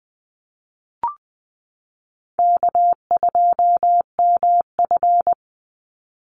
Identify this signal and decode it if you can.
Morse code, heard as a keyed tone, about 20 wpm, 700 Hz. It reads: K2MF
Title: Easy Call Signs Repeated 3x - 20wpm